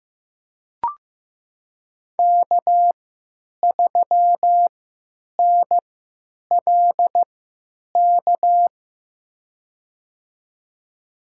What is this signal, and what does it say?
Morse code, heard as a keyed tone, about 15 wpm, spaced as for 9 wpm, 700 Hz. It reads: K3NLK
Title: USA Call Signs - 3x Character Spacing - 15wpm